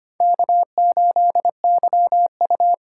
Morse code, heard as a keyed tone, about 25 wpm, 700 Hz. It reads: K8YU
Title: USA Call Signs - 4 Characters - 4x Character Spacing - 25wpm